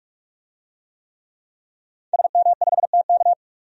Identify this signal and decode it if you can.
Morse code, heard as a keyed tone, about 46 wpm, 700 Hz. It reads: SM5TK
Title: DXCC-100 Call Signs - 5 Characters - 45wpm